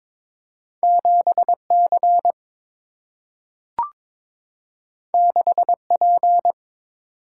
Morse code, heard as a keyed tone, about 22 wpm, 700 Hz. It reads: 7C 6P
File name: practice / 22wpm / Two Letter-Number Combinations - 22wpm